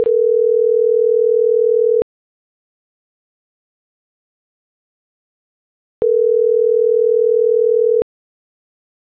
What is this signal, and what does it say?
Ringback tone (North America), 440 + 480 Hz.